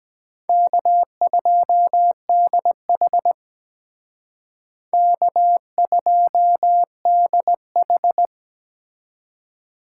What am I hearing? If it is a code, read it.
Morse code, heard as a keyed tone, about 20 wpm, 700 Hz. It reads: K2DH K2DH
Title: Speed Racing - USA Call Signs 17wpm